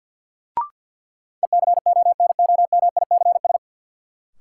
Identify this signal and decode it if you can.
Morse code, heard as a keyed tone, about 47 wpm, 700 Hz. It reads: ECONOMICS
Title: Sentences from Top 1000 Words - Review Part 3 - New Word Drill - 50wpm